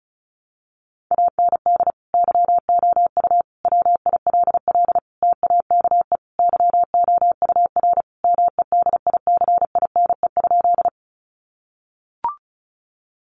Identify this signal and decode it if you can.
Morse code, heard as a keyed tone, about 34 wpm, 700 Hz. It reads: AND YOU WILL TAKE YOUR MEDICINE?
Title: Sentences from Top 900 Words 35wpm